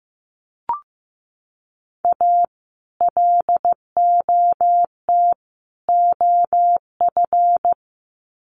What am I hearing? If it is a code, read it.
Morse code, heard as a keyed tone, about 15 wpm, 700 Hz. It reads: A LOT OF
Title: Sets of 3 Words 15wpm